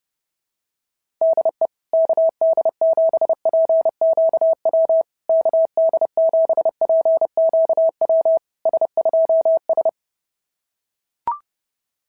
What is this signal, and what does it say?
Morse code, heard as a keyed tone, about 30 wpm, 650 Hz. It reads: DE KD7PQW KD7PQW S2S
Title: SOTA QSO Practice 30wpm